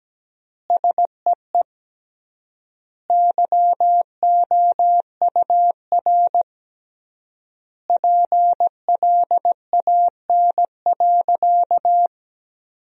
Morse code, heard as a keyed tone, about 17 wpm, 700 Hz. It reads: SEE YOUR PLAN.
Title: 3x Word-Spacing - Sentences from Top 500 Words - 17wpm